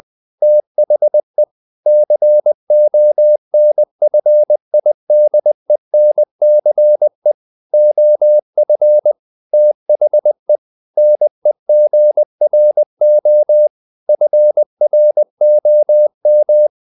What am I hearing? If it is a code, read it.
Morse code, heard as a keyed tone, about 20 wpm, 600 Hz. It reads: THE CONFIDENCE OF THE NEGRO FROM
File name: the old man and the sea part 2 20wpm